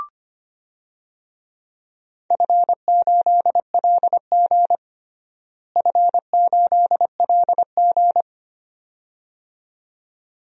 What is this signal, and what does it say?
Morse code, heard as a keyed tone, about 25 wpm, 700 Hz. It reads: F8LG F8LG
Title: DXCC-100 Call Signs - 4 Characters - Repeated 2x - 25wpm